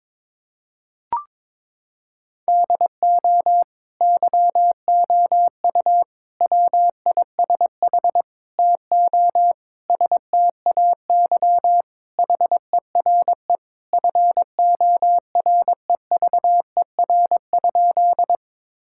Morse code, heard as a keyed tone, about 22 wpm, 700 Hz. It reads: DO YOU WISH TO STAY HERE FOREVER?
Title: Sentences from Top 1600 Words 22wpm